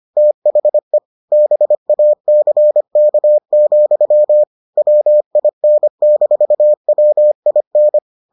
Morse code, heard as a keyed tone, about 25 wpm, 600 Hz. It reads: THE BACK, WIN-WIN